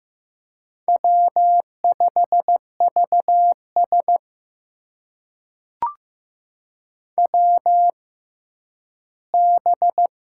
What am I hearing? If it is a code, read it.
Morse code, heard as a keyed tone, about 15 wpm, spaced as for 10 wpm, 700 Hz. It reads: W5VS WB
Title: USA Call Signs - 6x Character Spacing - 15wpm